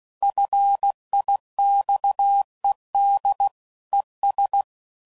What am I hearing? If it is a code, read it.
Morse code, heard as a keyed tone, about 16 wpm, 800 Hz. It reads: FIXED ES
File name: CW Code Groups @15WPM / QSOs / WA7QOX